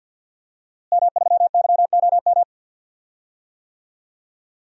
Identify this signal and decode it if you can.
Morse code, heard as a keyed tone, about 46 wpm, 700 Hz. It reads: M3YOM
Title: DXCC-100 Call Signs - 5 Characters - Repeated 3x - 50wpm